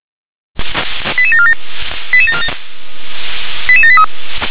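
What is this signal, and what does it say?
Five-tone selective call (ZVEI), heard as quick runs of five tones, tones of about 70 ms. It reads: 88747 88783
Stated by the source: ZVEI five-tone selective call received on an SDR.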